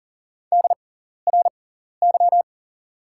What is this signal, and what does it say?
Morse code, heard as a keyed tone, about 40 wpm, spaced as for 14 wpm, 700 Hz. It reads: DRY
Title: 6x Character Spacing - Top 500 Words - 40wpm